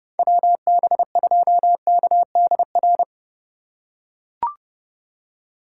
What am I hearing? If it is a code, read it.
Morse code, heard as a keyed tone, about 32 wpm, 700 Hz. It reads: WB2KDR